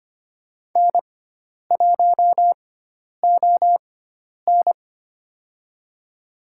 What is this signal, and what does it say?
Morse code, heard as a keyed tone, about 25 wpm, spaced as for 10 wpm, 700 Hz. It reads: N1ON
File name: Lesson 16g - Letter c - Call Signs 25wpm